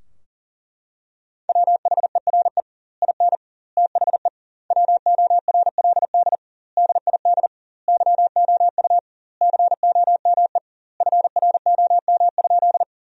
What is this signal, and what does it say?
Morse code, heard as a keyed tone, about 37 wpm, 700 Hz. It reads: WHERE IN THE WORLD DID YOU COME FROM?
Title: 2x Word-Spacing - Sentences from Top 200 Words - 40wpm